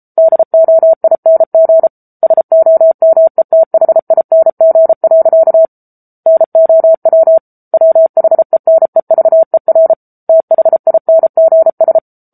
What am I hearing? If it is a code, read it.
Morse code, heard as a keyed tone, about 34 wpm, 650 Hz. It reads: DOING SOMETHING. NOW WHENEVER THINGS
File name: Book Two / 07-Chapter 7 - 33 WPM